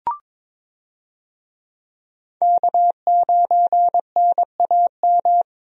Morse code, heard as a keyed tone, about 22 wpm, 700 Hz. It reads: K9NAM
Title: USA Call Signs - 5 Characters - Repeated 3x - 22wpm